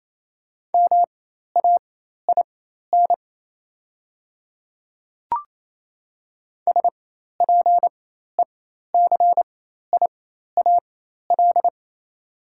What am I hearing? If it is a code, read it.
Morse code, heard as a keyed tone, about 28 wpm, spaced as for 13 wpm, 700 Hz. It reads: MAIN SPECIAL
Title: 4x Character Spacing - Top 500 Words - 28wpm